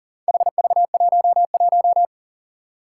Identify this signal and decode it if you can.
Morse code, heard as a keyed tone, about 40 wpm, 700 Hz. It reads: HV11
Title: Four Challenging Letter-Number Combinations 40wpm